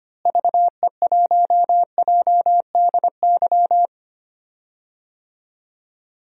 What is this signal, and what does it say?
Morse code, heard as a keyed tone, about 25 wpm, 700 Hz. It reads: VE1JDY
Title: DXCC-100 Call Signs - 6 Characters - Repeated 2x - 25wpm